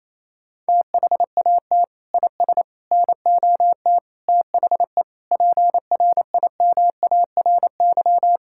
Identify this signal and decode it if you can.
Morse code, heard as a keyed tone, about 28 wpm, 700 Hz. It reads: THAT IS NOT THE PRIMARY
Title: Sentences from Top 1000 Words - Review Part 5 - 28wpm